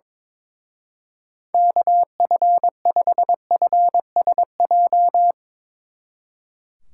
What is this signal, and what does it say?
Morse code, heard as a keyed tone, about 22 wpm, 700 Hz. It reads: KF5FSJ
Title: Speed Racing - USA 2x3 Call Signs 22wpm